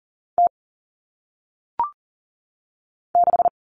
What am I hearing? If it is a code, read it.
Morse code, heard as a keyed tone, about 39 wpm, 700 Hz. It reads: T 6
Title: All Cut Numbers - 1-Digit 40wpm